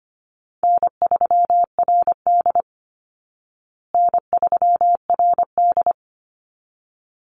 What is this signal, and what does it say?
Morse code, heard as a keyed tone, about 25 wpm, 700 Hz. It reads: N3RD N3RD
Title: Easy Call Signs Repeated 2x - 25wpm